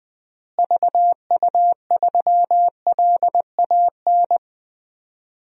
Morse code, heard as a keyed tone, about 21 wpm, 700 Hz. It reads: VU3LAN
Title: DXCC-100 Call Signs - 6 Characters - Repeated 3x - 20wpm